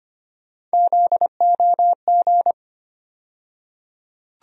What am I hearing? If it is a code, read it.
Morse code, heard as a keyed tone, about 25 wpm, 700 Hz. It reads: ZOG
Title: USA Call Sign Postfixes - 3 Letters 25wpm